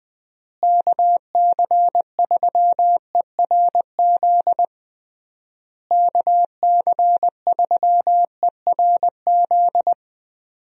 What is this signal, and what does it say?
Morse code, heard as a keyed tone, about 20 wpm, 700 Hz. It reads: KC3ERZ KC3ERZ
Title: USA Call Signs - 6 Characters - Repeated 3x - 20wpm